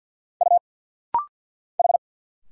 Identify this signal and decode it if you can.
Morse code, heard as a keyed tone, about 50 wpm, 700 Hz. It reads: U H
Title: Single Letters - Rapid-Fire 50wpm